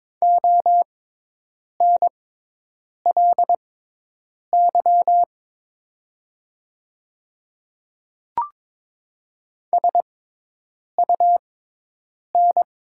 Morse code, heard as a keyed tone, about 22 wpm, spaced as for 8 wpm, 700 Hz. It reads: ONLY SUN
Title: 6x Character Spacing - Top 500 Words - 22wpm